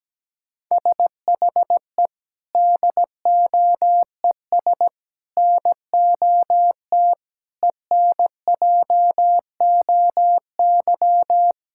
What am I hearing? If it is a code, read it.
Morse code, heard as a keyed tone, about 17 wpm, 700 Hz. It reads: SHE DOES NOT ENJOY
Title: Sentences from Top 1000 Words - Review Part 5 - 17wpm